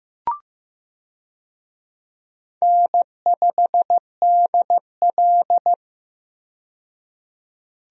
Morse code, heard as a keyed tone, about 15 wpm, 700 Hz. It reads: N5DL
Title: USA Call Signs - 4 Characters - 15wpm